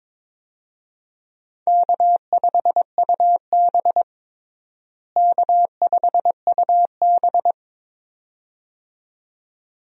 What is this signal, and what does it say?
Morse code, heard as a keyed tone, about 22 wpm, 700 Hz. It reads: K5UB K5UB
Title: USA Call Signs - 4 Characters - Repeated 2x - 22wpm